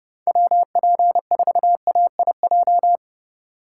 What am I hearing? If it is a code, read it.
Morse code, heard as a keyed tone, about 29 wpm, 700 Hz. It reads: WP4AIJ